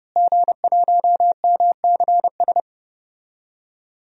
Morse code, heard as a keyed tone, about 30 wpm, 700 Hz. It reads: G1MCS
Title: DXCC-100 Call Signs - 5 Characters - Repeated 3x - 30wpm